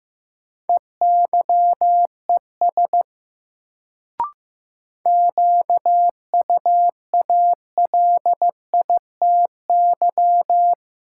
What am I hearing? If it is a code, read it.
Morse code, heard as a keyed tone, about 15 wpm, 700 Hz. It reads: EYES QUALITY
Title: Sentences from Top 1000 Words - Review Part 4 - New Word Drill - 15wpm